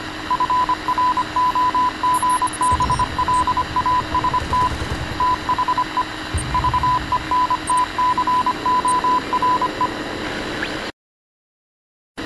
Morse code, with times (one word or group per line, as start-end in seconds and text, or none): 0.3-4.7 FROGBLAST
5.2-6.0 THE
6.5-9.9 VENTCORE